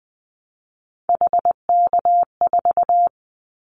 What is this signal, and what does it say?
Morse code, heard as a keyed tone, about 20 wpm, 700 Hz. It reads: HK4
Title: Callsign Prefixes for Top 100 DXCC - 20wpm